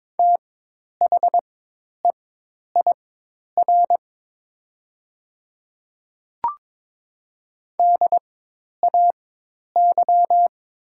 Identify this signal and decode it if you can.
Morse code, heard as a keyed tone, about 22 wpm, spaced as for 10 wpm, 700 Hz. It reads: THEIR DAY
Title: 4x Character Spacing - Top 100 Words - 22wpm